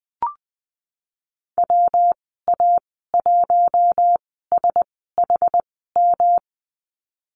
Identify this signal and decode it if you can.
Morse code, heard as a keyed tone, about 20 wpm, spaced as for 15 wpm, 700 Hz. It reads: WA1SHM
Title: 147 Lesson 24g - Letter y - Call Signs 20wpm